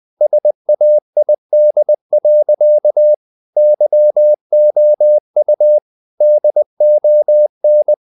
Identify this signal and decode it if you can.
Morse code, heard as a keyed tone, about 20 wpm, 600 Hz. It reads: SAID. YOU DON
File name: the old man and the sea / the old man and the sea part 1 20wpm